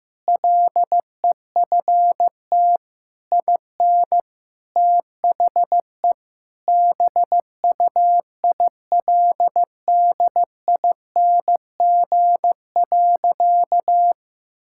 Morse code, heard as a keyed tone, about 16 wpm, 700 Hz. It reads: LEFT IN THE BUILDING.
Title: Sentences from Top 1000 Words - Review Part 5 - 15wpm